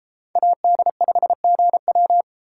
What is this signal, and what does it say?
Morse code, heard as a keyed tone, about 32 wpm, 700 Hz. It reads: AD5GW